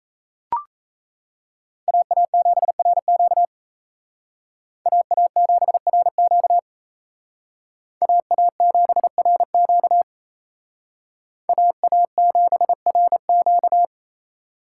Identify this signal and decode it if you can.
Morse code, heard as a keyed tone, about 34 wpm, 700 Hz. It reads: AA7RQ AA7RQ AA7RQ AA7RQ
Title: Speed Racing - USA 2x2 Call Signs 28wpm